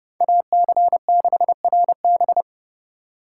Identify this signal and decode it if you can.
Morse code, heard as a keyed tone, about 29 wpm, 700 Hz. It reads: AC6RB